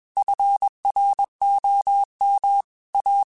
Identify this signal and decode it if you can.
Morse code, heard as a keyed tone, about 21 wpm, 800 Hz. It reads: FROM A
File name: QSOs / QSOs-mp3s / kc7sac@20